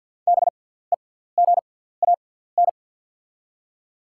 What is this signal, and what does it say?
Morse code, heard as a keyed tone, about 53 wpm, spaced as for 17 wpm, 700 Hz. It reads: BEGAN
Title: 6x Character Spacing - Top 300 Words - 50wpm